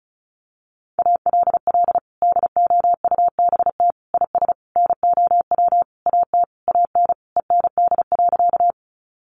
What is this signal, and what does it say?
Morse code, heard as a keyed tone, about 34 wpm, 700 Hz. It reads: ALL DOUBT IS NOW AT AN END.
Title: Sentences from Top 1500 Words 35wpm